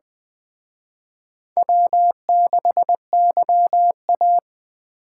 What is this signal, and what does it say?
Morse code, heard as a keyed tone, about 21 wpm, 700 Hz. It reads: W6YA